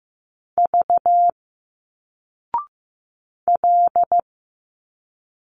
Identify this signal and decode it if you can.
Morse code, heard as a keyed tone, about 15 wpm, 700 Hz. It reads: V L
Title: Single Letters 15wpm